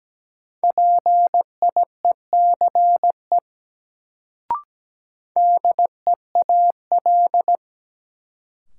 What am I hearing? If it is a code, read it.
Morse code, heard as a keyed tone, about 17 wpm, 700 Hz. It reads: PIECE DEAL
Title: Sets of 3 Words - New Word Drill - 17wpm